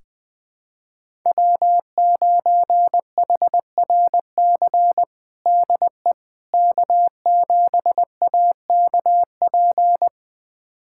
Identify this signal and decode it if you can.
Morse code, heard as a keyed tone, about 20 wpm, 700 Hz. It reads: W9HRC DE K7AKP
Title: USA Callsign DE Callsign - 5 Characters - 20wpm